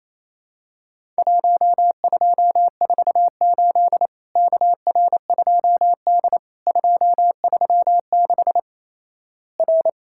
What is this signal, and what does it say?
Morse code, heard as a keyed tone, about 28 wpm, 700 Hz. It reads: 1248 KR2D 236 R
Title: Worked All Europe DX Contest - QTC Exchange - 28wpm